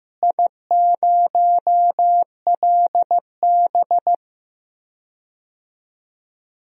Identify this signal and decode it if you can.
Morse code, heard as a keyed tone, about 15 wpm, 700 Hz. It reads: I0LB
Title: DXCC-100 Call Signs - 4 Characters - Repeated 2x - 15wpm